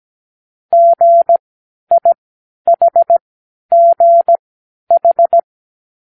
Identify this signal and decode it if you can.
Morse code, heard as a keyed tone, about 17 wpm, spaced as for 10 wpm, 680 Hz. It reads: GIHGH
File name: k7qo / Lesson15(A-J)